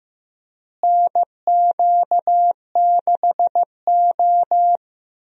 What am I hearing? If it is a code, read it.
Morse code, heard as a keyed tone, about 15 wpm, 700 Hz. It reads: NQ6O
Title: USA Call Signs - 4 Characters - Repeated 3x - 15wpm